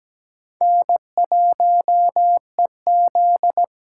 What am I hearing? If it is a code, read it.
Morse code, heard as a keyed tone, about 17 wpm, 700 Hz. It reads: N1EZ
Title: USA Call Signs - 4 Characters - 3x Character Spacing - 17wpm